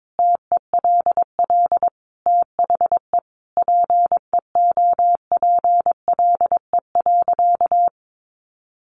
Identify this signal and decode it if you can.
Morse code, heard as a keyed tone, about 22 wpm, 700 Hz. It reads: TELL THE PEOPLE.
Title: Sentences from Top 1200 Words 22wpm